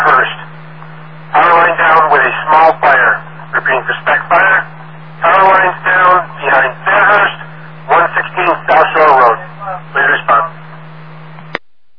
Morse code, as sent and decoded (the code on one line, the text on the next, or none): none